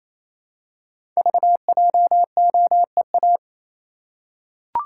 Morse code, heard as a keyed tone, about 27 wpm, 700 Hz. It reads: VJOEA
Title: Code Groups - 5 Letters - 28wpm